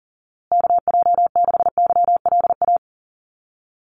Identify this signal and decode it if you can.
Morse code, heard as a keyed tone, about 40 wpm, 700 Hz. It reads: KJ6YLA
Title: USA Call Signs Repeated 2x 40wpm